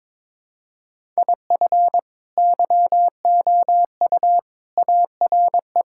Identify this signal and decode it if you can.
Morse code, heard as a keyed tone, about 22 wpm, 700 Hz. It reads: IF YOU ARE
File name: practice / 22wpm / Sets of 6 Words - 22wpm